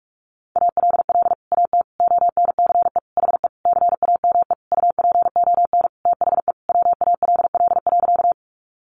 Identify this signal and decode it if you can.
Morse code, heard as a keyed tone, about 44 wpm, 700 Hz. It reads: ALL AT ONCE HE CAME UPON THE WALL.
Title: Sentences from Top 1100 Words 45wpm